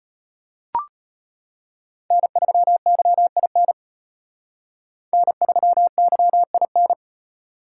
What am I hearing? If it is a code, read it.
Morse code, heard as a keyed tone, about 29 wpm, 700 Hz. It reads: N3YIN N3YIN
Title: Speed Racing - USA 1x3 Call Signs 25wpm